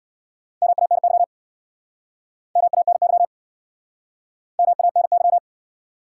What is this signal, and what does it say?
Morse code, heard as a keyed tone, about 64 wpm, 700 Hz. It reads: MANY MANY MANY